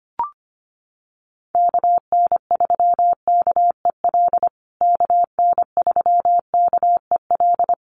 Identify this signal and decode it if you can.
Morse code, heard as a keyed tone, about 25 wpm, 700 Hz. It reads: KN3KEL KN3KEL
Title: USA Call Signs Repeated 3x 25wpm